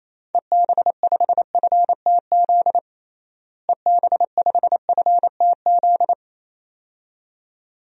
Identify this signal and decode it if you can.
Morse code, heard as a keyed tone, about 27 wpm, 700 Hz. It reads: EB5FTZ EB5FTZ
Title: DXCC-100 Call Signs - 6 Characters - Repeated 2x - 28wpm